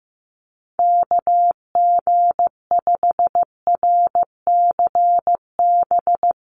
Morse code, heard as a keyed tone, about 15 wpm, 700 Hz. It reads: KG5RCB
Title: USA Call Signs 15wpm